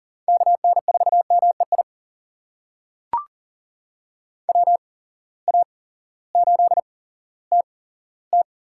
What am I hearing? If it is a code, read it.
Morse code, heard as a keyed tone, about 40 wpm, spaced as for 18 wpm, 700 Hz. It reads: KN4MEI WA8TT
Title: USA Call Signs - 6 Characters - 8x Character Spacing - 40wpm